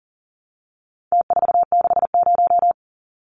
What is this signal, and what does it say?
Morse code, heard as a keyed tone, about 40 wpm, 700 Hz. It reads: T460
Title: Numbers - Contest Style - 4-Digit 40wpm